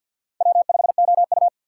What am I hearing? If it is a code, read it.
Morse code, heard as a keyed tone, about 53 wpm, 700 Hz. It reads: W5OU
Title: USA Call Signs - 3x Character Spacing - 50wpm